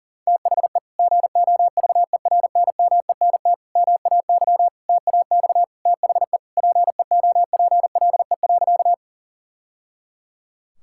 Morse code, heard as a keyed tone, about 40 wpm, 700 Hz. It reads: THE GOVERNMENT MAY TAX THE PEOPLE.
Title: Sentences from Top 1000 Words - Review Part 5 - 40wpm